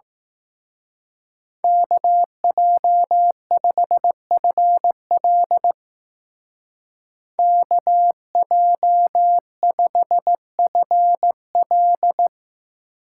Morse code, heard as a keyed tone, about 17 wpm, 700 Hz. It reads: KJ5FL KJ5FL
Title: Speed Racing - USA 2x2 Call Signs 15wpm